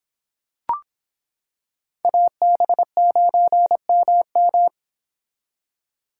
Morse code, heard as a keyed tone, about 25 wpm, 700 Hz. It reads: AB9MM